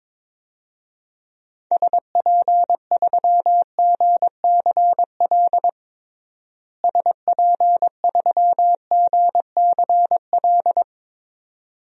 Morse code, heard as a keyed tone, about 22 wpm, 700 Hz. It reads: SP3GCL SP3GCL
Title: DXCC-100 Call Signs - 6 Characters - Repeated 2x - 22wpm